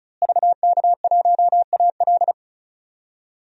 Morse code, heard as a keyed tone, about 34 wpm, 700 Hz. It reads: VK1AL
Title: DXCC-100 Call Signs - 5 Characters - Repeated 3x - 35wpm